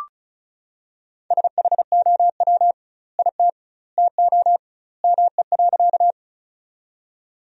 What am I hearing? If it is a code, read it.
Morse code, heard as a keyed tone, about 36 wpm, 700 Hz. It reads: SHOW IT TO ME.
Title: 2x Word-Spacing - Sentences from Top 200 Words - 35wpm